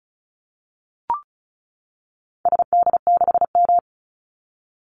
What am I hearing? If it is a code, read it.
Morse code, heard as a keyed tone, about 36 wpm, 700 Hz. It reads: SD6M
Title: Call Signs - 35wpm